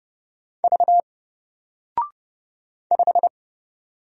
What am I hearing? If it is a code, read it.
Morse code, heard as a keyed tone, about 30 wpm, 700 Hz. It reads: V 5
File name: Single-Letter-Number-30wpm